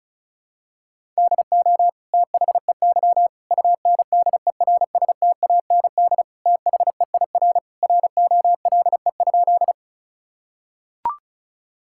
Morse code, heard as a keyed tone, about 34 wpm, 700 Hz. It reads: DO THEY UNDERSTAND THEIR ROLE?
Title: Sentences from Top 1000 Words - Review Part 6 - 35wpm